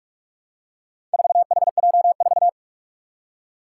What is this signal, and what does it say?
Morse code, heard as a keyed tone, about 44 wpm, 700 Hz. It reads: 4HJ4
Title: Four Challenging Letter-Number Combinations 45wpm